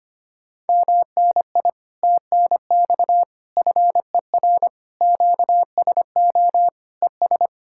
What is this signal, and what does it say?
Morse code, heard as a keyed tone, about 24 wpm, 700 Hz. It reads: MNI TNX FER QSO ES
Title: Speed-Racing Common QSO Phrases - 25wpm